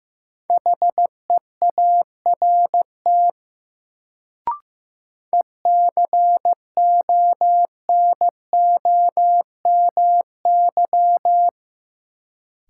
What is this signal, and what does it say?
Morse code, heard as a keyed tone, about 15 wpm, 700 Hz. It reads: HEART ECONOMY
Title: Sentences from Top 1200 Words - New Word Drill Alt - 15wpm